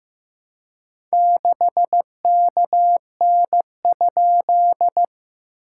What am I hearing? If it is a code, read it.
Morse code, heard as a keyed tone, about 15 wpm, 700 Hz. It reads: 6KN?
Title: SOTA QSO Practice 15wpm